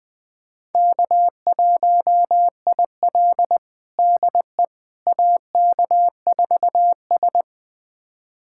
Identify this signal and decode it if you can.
Morse code, heard as a keyed tone, about 20 wpm, 700 Hz. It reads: K1IL DE AK4S
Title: USA Callsign DE Callsign - 4 Characters - 20wpm